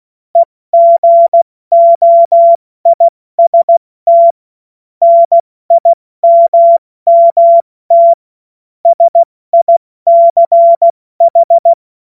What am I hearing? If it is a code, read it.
Morse code, heard as a keyed tone, about 16 wpm, 680 Hz. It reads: EGOIST NIMMT SICH
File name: Morsekurs / Witze / Witz 16. Februar 2021, 70 F80